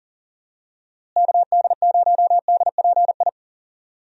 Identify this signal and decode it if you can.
Morse code, heard as a keyed tone, about 38 wpm, 700 Hz. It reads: KD0DPI
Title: USA Call Signs - 6 Characters - Repeated 2x - 40wpm